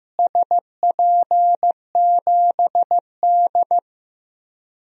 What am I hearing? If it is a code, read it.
Morse code, heard as a keyed tone, about 15 wpm, 700 Hz. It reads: SP7D
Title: DXCC-100 Call Signs - 4 Characters - Repeated 3x - 15wpm